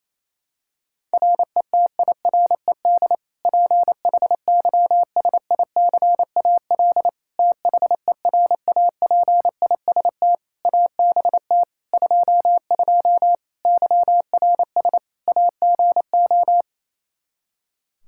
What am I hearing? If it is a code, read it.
Morse code, heard as a keyed tone, about 28 wpm, 700 Hz. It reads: RETIRED PHYSICAL THERAPIST ABT 22 YRS AGO
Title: Practice QSO - Exchange 3 - More Info - 28wpm